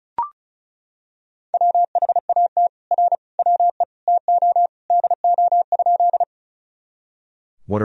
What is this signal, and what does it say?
Morse code, heard as a keyed tone, about 34 wpm, 700 Hz. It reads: WHAT R WE TO DO?
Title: Sentences from Top 100 Words and Abbreviations 35wpm